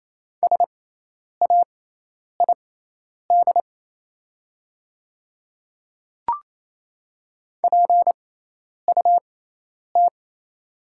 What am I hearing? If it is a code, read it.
Morse code, heard as a keyed tone, about 28 wpm, spaced as for 10 wpm, 700 Hz. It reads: SAID PUT
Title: 6x Character Spacing - Top 200 Words - 28wpm